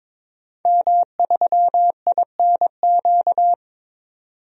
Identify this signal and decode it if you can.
Morse code, heard as a keyed tone, about 21 wpm, 700 Hz. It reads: M3INQ
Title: DXCC-100 Call Signs - 5 Characters - 22wpm